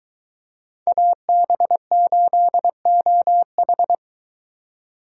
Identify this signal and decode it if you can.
Morse code, heard as a keyed tone, about 23 wpm, 700 Hz. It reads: AB8OH